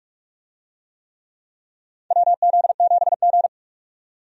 Morse code, heard as a keyed tone, about 46 wpm, 700 Hz. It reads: WZ7G
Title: USA Call Signs - 4 Characters - 45wpm